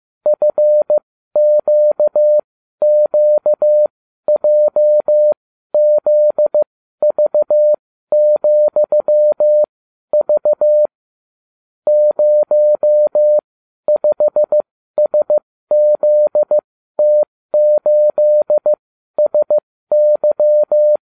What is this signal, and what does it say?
Morse code, heard as a keyed tone, about 15 wpm, spaced as for 12 wpm, 600 Hz. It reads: FQQJZV,V 05SZT8SY